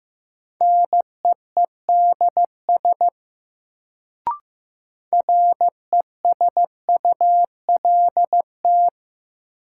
Morse code, heard as a keyed tone, about 15 wpm, 700 Hz. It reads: NEEDS RESULT
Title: Sets of 3 Words - New Word Drill - 15wpm